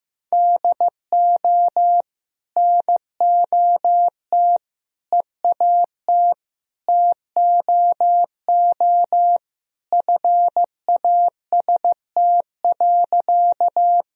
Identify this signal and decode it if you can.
Morse code, heard as a keyed tone, about 15 wpm, 700 Hz. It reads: DO NOT EAT TOO FAST.
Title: Sentences from Top 1000 Words - Review Part 2 - 15wpm